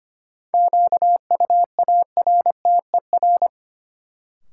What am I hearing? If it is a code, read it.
Morse code, heard as a keyed tone, about 25 wpm, 700 Hz. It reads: QUARTER
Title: Sentences from Top 1300 Words - New Word Drill Alt - 25wpm